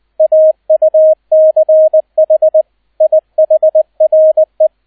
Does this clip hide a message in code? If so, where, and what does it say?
Yes: Morse code from 0.2 to 4.7 s, heard as a keyed tone, about 20 wpm, 630 Hz, with radio noise. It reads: AUCH IHRE